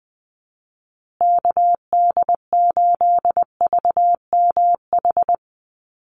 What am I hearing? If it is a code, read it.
Morse code, heard as a keyed tone, about 20 wpm, 700 Hz. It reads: KD8VMH
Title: USA Call Signs Repeated 3x 20wpm